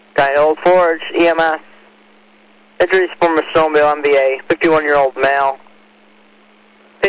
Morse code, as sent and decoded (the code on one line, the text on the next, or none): none